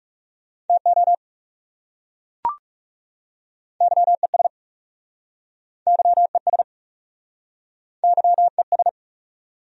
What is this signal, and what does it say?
Morse code, heard as a keyed tone, about 40 wpm, 700 Hz. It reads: TO YES YES YES